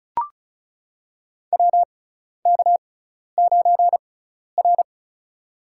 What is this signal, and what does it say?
Morse code, heard as a keyed tone, about 36 wpm, spaced as for 12 wpm, 700 Hz. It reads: WK9R
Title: USA Call Signs - 6x Character Spacing - 35wpm